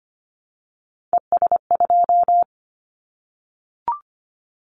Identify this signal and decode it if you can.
Morse code, heard as a keyed tone, about 25 wpm, 700 Hz. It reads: ES2
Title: Callsign Prefixes for Top 100 DXCC - 25wpm